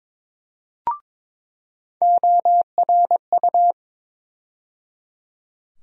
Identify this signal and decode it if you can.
Morse code, heard as a keyed tone, about 22 wpm, 700 Hz. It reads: ORU
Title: USA Call Sign Postfixes - 3 Letters 22wpm